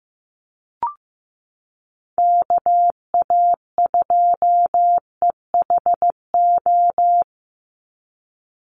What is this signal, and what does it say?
Morse code, heard as a keyed tone, about 15 wpm, 700 Hz. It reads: KA2EHO
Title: USA Call Signs 15wpm